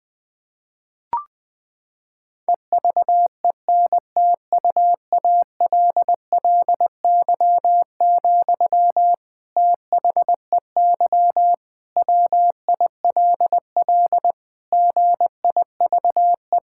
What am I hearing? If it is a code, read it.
Morse code, heard as a keyed tone, about 19 wpm, 700 Hz. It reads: EVENTUALLY, THEY WILL GIVE
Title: Sentences from Top 1000 Words - Review Part 6 - 20wpm